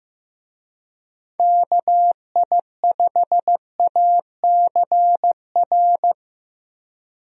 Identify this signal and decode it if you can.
Morse code, heard as a keyed tone, about 14 wpm, 700 Hz. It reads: KI5ACR
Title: USA Call Signs - 6 Characters - Repeated 2x - 15wpm